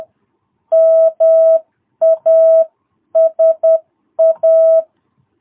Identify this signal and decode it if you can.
Morse code, heard as a keyed tone, about 10 wpm, 660 Hz. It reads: MASA
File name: jh8cbh / 2022cw